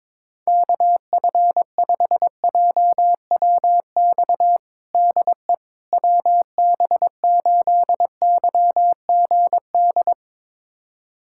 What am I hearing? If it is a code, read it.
Morse code, heard as a keyed tone, about 22 wpm, 700 Hz. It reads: KF5JWX DE WB8YGD
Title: USA Callsign DE Callsign 22wpm